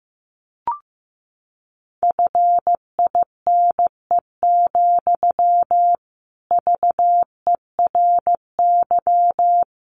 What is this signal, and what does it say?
Morse code, heard as a keyed tone, about 15 wpm, 700 Hz. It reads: FINE, VERY
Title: Sentences from Top 500 Words 15wpm